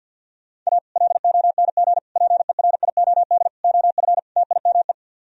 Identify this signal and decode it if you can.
Morse code, heard as a keyed tone, about 64 wpm, 700 Hz. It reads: A LONG PERIOD OF TIME